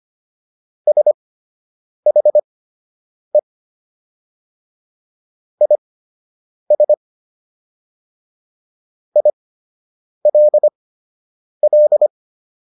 Morse code, heard as a keyed tone, about 25 wpm, spaced as for 8 wpm, 600 Hz. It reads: SHE IS ILL